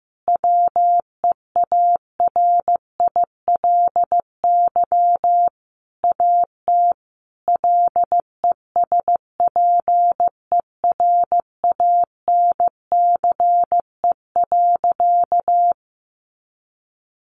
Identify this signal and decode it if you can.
Morse code, heard as a keyed tone, about 15 wpm, 700 Hz. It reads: WEARILY AT LESPERANCE.
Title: A Sound Of Thunder 15wpm